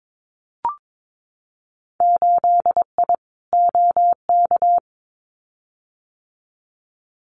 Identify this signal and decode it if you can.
Morse code, heard as a keyed tone, about 22 wpm, 700 Hz. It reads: 8I OK
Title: Winter Field Day Exchanges Encore 22wpm